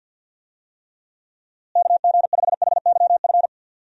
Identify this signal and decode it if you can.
Morse code, heard as a keyed tone, about 51 wpm, 700 Hz. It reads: KG5HYF
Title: USA Call Signs - 6 Characters - Repeated 3x - 50wpm